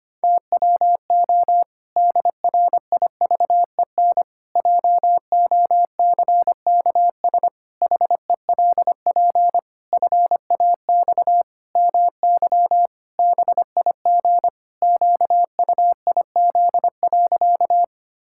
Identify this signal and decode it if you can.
Morse code, heard as a keyed tone, about 25 wpm, 700 Hz. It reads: TWO DRIVEN JOCKS HELP FAX MY BIG QUIZ.
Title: Pangram Challenge v2 - 25wpm